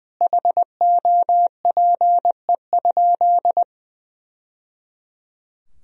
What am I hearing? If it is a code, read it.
Morse code, heard as a keyed tone, about 20 wpm, 700 Hz. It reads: HOPE?
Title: Sentences from Top 1000 Words - Review Part 3 - 20wpm